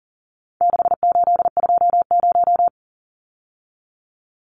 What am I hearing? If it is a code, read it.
Morse code, heard as a keyed tone, about 37 wpm, 700 Hz. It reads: 6820
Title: Numbers - 4-Digit 40wpm